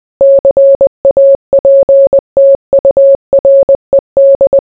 Morse code, heard as a keyed tone, about 20 wpm, 550 Hz. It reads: CAPTURED